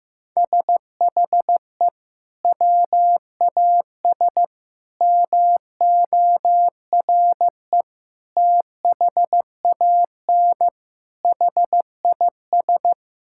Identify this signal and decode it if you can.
Morse code, heard as a keyed tone, about 15 wpm, 700 Hz. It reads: SHE WAS MORE THAN HIS
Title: Sentences from Top 1600 Words 15wpm